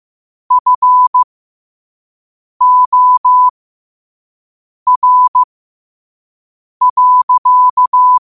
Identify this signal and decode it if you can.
Morse code, heard as a keyed tone, about 15 wpm, spaced as for 5 wpm, 1000 Hz. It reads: FOR.
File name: qotdfiles / 5wpm